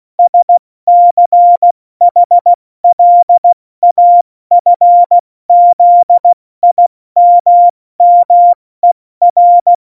Morse code, heard as a keyed tone, about 16 wpm, 700 Hz. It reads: SCHLAFZIMMER